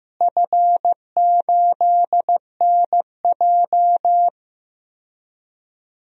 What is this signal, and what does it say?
Morse code, heard as a keyed tone, about 15 wpm, 700 Hz. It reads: F8NJ
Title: DXCC-100 Call Signs - 4 Characters - Repeated 3x - 15wpm